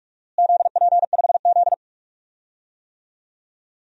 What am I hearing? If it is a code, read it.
Morse code, heard as a keyed tone, about 46 wpm, 690 Hz. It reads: ZP5Z
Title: DXCC-100 Call Signs - 4 Characters - Repeated 3x - 45wpm